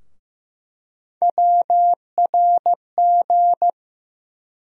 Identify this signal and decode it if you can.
Morse code, heard as a keyed tone, about 15 wpm, 700 Hz. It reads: WRG
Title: USA Call Sign Postfixes - 3 Letters 15wpm